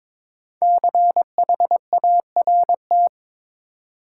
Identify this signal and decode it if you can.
Morse code, heard as a keyed tone, about 22 wpm, 700 Hz. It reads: CHART
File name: 5 Letter Words - First 3 Letters Given - 22wpm